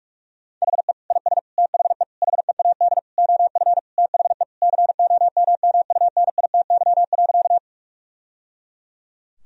Morse code, heard as a keyed tone, about 46 wpm, 700 Hz. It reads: HE IS THE HEAD OF THE COMMUNITY.
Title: Sentences from Top 1000 Words - Review Part 4 - 45wpm